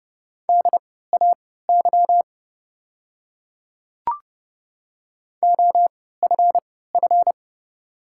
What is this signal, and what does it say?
Morse code, heard as a keyed tone, about 30 wpm, spaced as for 17 wpm, 690 Hz. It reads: DAY OFF
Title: 3x Character Spacing - Top 200 Words - 30wpm